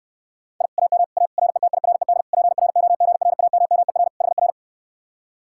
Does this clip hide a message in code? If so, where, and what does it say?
Yes: Morse code from 0.6 to 4.5 s, heard as a keyed tone, about 68 wpm, 700 Hz. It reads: I AM A RETIRED PROGRAMMER HR